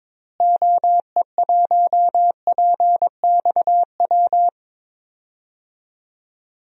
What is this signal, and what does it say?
Morse code, heard as a keyed tone, about 22 wpm, 700 Hz. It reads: OE1PXW
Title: DXCC-100 Call Signs - 6 Characters - Repeated 2x - 22wpm